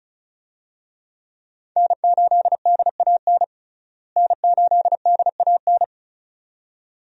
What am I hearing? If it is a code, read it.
Morse code, heard as a keyed tone, about 34 wpm, 700 Hz. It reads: N8DAN N8DAN
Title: USA Call Signs - 5 Characters - Repeated 2x - 35wpm